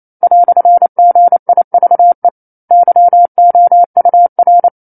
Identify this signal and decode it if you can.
Morse code, heard as a keyed tone, about 28 wpm, 700 Hz. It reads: "GIVE YOUR
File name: 09-Chapter-09 28 WPM